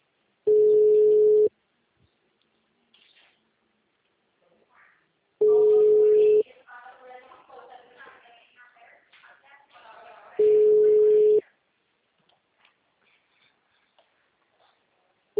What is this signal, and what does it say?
Ringback tone (Europe, 425 Hz), with speech.